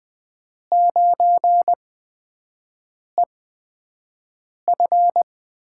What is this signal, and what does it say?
Morse code, heard as a keyed tone, about 20 wpm, spaced as for 5 wpm, 700 Hz. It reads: 9EF
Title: USA Call Signs - 8x Character Spacing - 20wpm